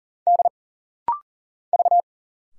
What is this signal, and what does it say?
Morse code, heard as a keyed tone, about 40 wpm, 700 Hz. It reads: D V
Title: Single Letters - Rapid-Fire 40wpm